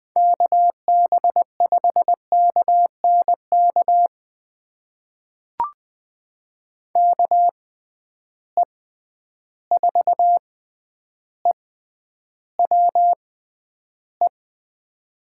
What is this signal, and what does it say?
Morse code, heard as a keyed tone, about 20 wpm, spaced as for 10 wpm, 700 Hz. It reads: KB5KNK KE4EWE
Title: USA Call Signs - 6 Characters - 6x Character Spacing - 20wpm